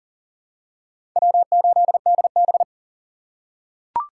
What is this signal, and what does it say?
Morse code, heard as a keyed tone, about 39 wpm, 700 Hz. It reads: W8DB
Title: USA Call Signs - 4 Characters - 2x Character Spacing - 40wpm